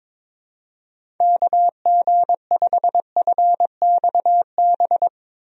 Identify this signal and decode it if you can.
Morse code, heard as a keyed tone, about 22 wpm, 700 Hz. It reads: KG5FXB
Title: USA Call Signs - 6 Characters - Repeated 3x - 22wpm